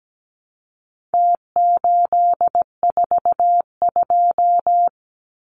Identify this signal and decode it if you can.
Morse code, heard as a keyed tone, about 17 wpm, 700 Hz. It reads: T842
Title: Numbers - Contest Style - 4-Digit 17wpm